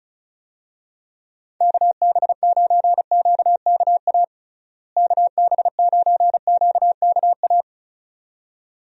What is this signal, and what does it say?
Morse code, heard as a keyed tone, about 36 wpm, 700 Hz. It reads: KB9QKA KB9QKA
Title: USA Call Signs - 6 Characters - Repeated 2x - 35wpm